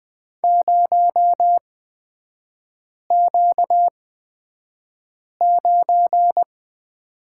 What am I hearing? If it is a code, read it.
Morse code, heard as a keyed tone, about 20 wpm, spaced as for 5 wpm, 700 Hz. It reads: 0Q9
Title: 250 Lesson 40e - BK - Three Characters 20wpm